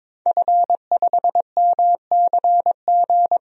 Morse code, heard as a keyed tone, about 22 wpm, 700 Hz. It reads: F5MCG
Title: DXCC-100 Call Signs - 5 Characters - 22wpm